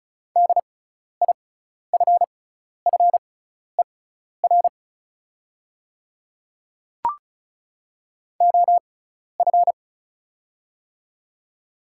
Morse code, heard as a keyed tone, about 34 wpm, spaced as for 12 wpm, 700 Hz. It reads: DIFFER OF
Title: 6x Character Spacing - Top 200 Words - 35wpm